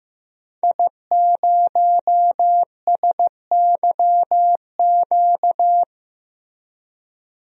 Morse code, heard as a keyed tone, about 15 wpm, 700 Hz. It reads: I0SYQ
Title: DXCC-100 Call Signs - 5 Characters - Repeated 3x - 15wpm